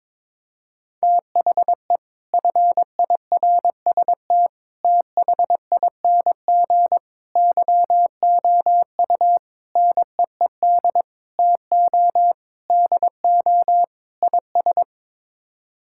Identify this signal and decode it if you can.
Morse code, heard as a keyed tone, about 22 wpm, 700 Hz. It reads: THE FIRST THING YOU NEED TO DO IS
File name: practice / 22wpm / Sets of 8 Words - 22wpm